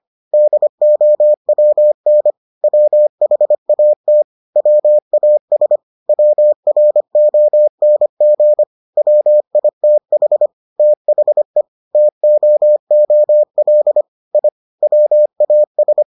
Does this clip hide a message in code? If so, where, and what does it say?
Yes: Morse code from 0.3 to 16.0 s, heard as a keyed tone, about 25 wpm, 600 Hz. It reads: DOWN WHAT WAS WRONG WITH THE TOOL I WAS